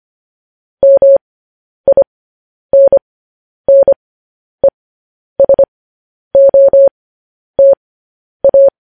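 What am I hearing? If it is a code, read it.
Morse code, heard as a keyed tone, about 25 wpm, spaced as for 10 wpm, 550 Hz. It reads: MINNESOTA
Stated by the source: Morse code news headlines sent at 10 wpm.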